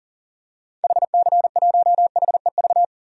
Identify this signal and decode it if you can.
Morse code, heard as a keyed tone, about 40 wpm, 700 Hz. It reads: HC1HEV
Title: DXCC-100 Call Signs - 6 Characters - 40wpm